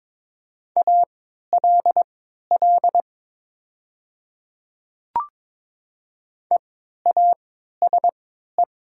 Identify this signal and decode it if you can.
Morse code, heard as a keyed tone, about 22 wpm, spaced as for 13 wpm, 700 Hz. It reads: ALL EASE
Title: 3x Character Spacing - Top 300 Words - 22wpm